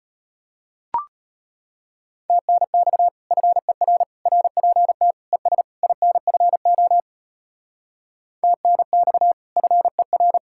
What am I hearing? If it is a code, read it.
Morse code, heard as a keyed tone, about 33 wpm, 700 Hz. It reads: TNX FER RPT ES INFO TNX FER